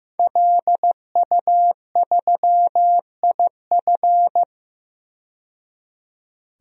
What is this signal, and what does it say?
Morse code, heard as a keyed tone, about 15 wpm, 700 Hz. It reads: LU3IF
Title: DXCC-100 Call Signs - 5 Characters - 15wpm